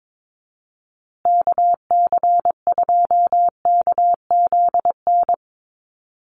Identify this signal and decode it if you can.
Morse code, heard as a keyed tone, about 22 wpm, 700 Hz. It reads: KC2KZN
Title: USA Call Signs Repeated 3x 22wpm